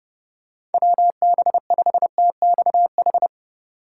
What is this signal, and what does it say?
Morse code, heard as a keyed tone, about 31 wpm, 700 Hz. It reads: WB5TXH
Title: USA Call Signs - 6 Characters - 8x Character Spacing - 30wpm